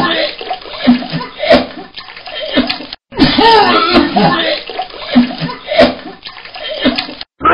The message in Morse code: none